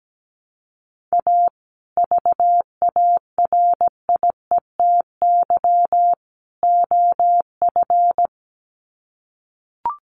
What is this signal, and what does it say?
Morse code, heard as a keyed tone, about 17 wpm, 700 Hz. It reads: A VARIETY OF
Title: Sets of 3 Words 17wpm